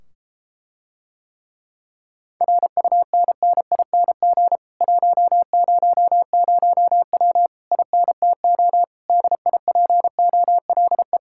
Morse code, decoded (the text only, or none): RUNNING 100W INTO DIPOLE